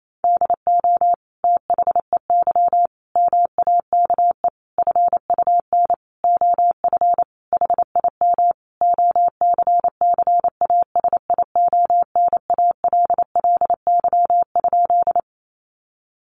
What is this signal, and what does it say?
Morse code, heard as a keyed tone, about 28 wpm, 700 Hz. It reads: DO THEY MAKE FUN OF HIM OCCASIONALLY?
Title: Sentences from Top 1300 Words 28wpm